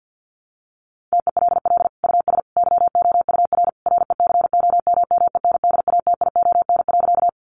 Morse code, heard as a keyed tone, about 50 wpm, 700 Hz. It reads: TELL US YOUR RECOMMENDATION.
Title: Sentences from Top 1400 Words 50wpm